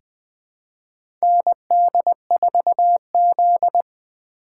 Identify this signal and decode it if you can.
Morse code, heard as a keyed tone, about 21 wpm, 700 Hz. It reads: ND4Z
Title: USA Call Signs - 4 Characters - 3x Character Spacing - 20wpm